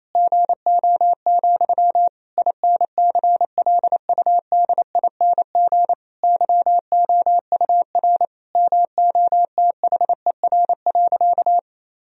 Morse code, heard as a keyed tone, about 28 wpm, 700 Hz. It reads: GO, INCLUDING YOUR MOTHER.
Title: Sentences from Top 1000 Words - Review Part 4 - 28wpm